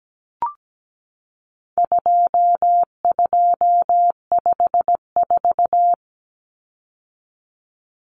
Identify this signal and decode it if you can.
Morse code, heard as a keyed tone, about 17 wpm, 700 Hz. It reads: 2254
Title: Common Cut Numbers - 4-Digit 17wpm